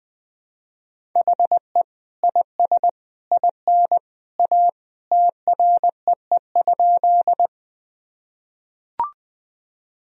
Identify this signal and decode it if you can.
Morse code, heard as a keyed tone, about 21 wpm, 700 Hz. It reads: HE IS IN A TREE?
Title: Sentences from Top 1000 Words - Review Part 1 - 20wpm